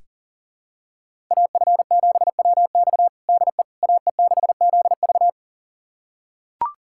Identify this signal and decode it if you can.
Morse code, heard as a keyed tone, about 40 wpm, 700 Hz. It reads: AF7WX DE AE6ZV
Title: USA Callsign DE Callsign - 5 Characters - 40wpm